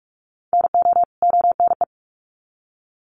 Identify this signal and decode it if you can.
Morse code, heard as a keyed tone, about 44 wpm, 700 Hz. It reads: NO ONE
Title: Sets of 2 Words 45wpm